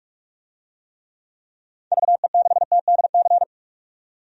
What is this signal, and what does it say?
Morse code, heard as a keyed tone, about 47 wpm, 700 Hz. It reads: VE6TDC
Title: DXCC-100 Call Signs - 6 Characters - 45wpm